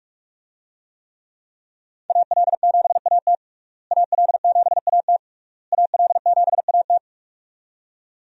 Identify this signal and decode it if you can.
Morse code, heard as a keyed tone, about 46 wpm, 700 Hz. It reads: AL7AT AL7AT AL7AT
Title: USA Call Signs - 5 Characters - Repeated 3x - 45wpm